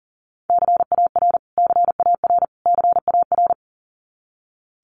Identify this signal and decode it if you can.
Morse code, heard as a keyed tone, about 40 wpm, 700 Hz. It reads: CAR CAR CAR
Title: Top 500 Words Repeated 40wpm